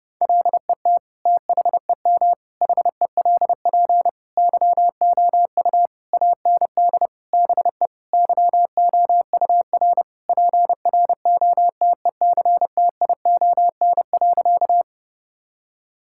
Morse code, heard as a keyed tone, about 30 wpm, 700 Hz. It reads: LET THEM HELP YOU AND BE YOUR PROTECTION.
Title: Sentences from Top 1600 Words 30wpm